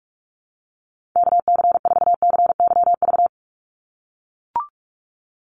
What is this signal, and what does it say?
Morse code, heard as a keyed tone, about 44 wpm, 700 Hz. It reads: KC4CYV
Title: USA Call Signs 45wpm